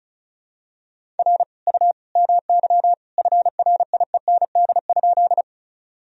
Morse code, heard as a keyed tone, about 34 wpm, 700 Hz. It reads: R U MY FRIEND?
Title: Sentences from Top 300 Words and Abbreviations 35wpm